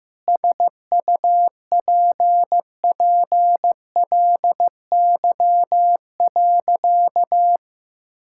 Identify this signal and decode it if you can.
Morse code, heard as a keyed tone, about 15 wpm, 700 Hz. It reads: SUPPLY.
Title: Sentences from Top 1000 Words - Review Part 5 - 15wpm